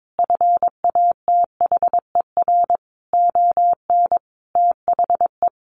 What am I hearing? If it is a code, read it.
Morse code, heard as a keyed tone, about 22 wpm, 700 Hz. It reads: FATHER ON THE
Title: Sentences from Top 500 Words 22wpm